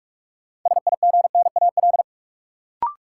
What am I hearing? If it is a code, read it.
Morse code, heard as a keyed tone, about 44 wpm, 700 Hz. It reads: SIGNAL
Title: Sentences from Top 1400 Words - New Word Drill - 45wpm